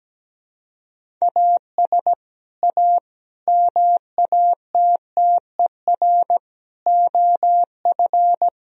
Morse code, heard as a keyed tone, about 17 wpm, 700 Hz. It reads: AS A MATTER OF
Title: Sentences from Top 1000 Words - Review Part 1 - 17wpm